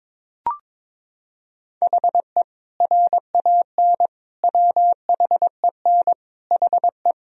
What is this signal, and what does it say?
Morse code, heard as a keyed tone, about 22 wpm, 700 Hz. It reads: HE RAN WHEN HE
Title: Sentences from Top 1000 Words - Review Part 3 - 22wpm